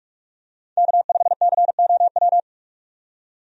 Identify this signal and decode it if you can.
Morse code, heard as a keyed tone, about 46 wpm, 700 Hz. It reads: K5COW
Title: USA Call Signs - 5 Characters - 45wpm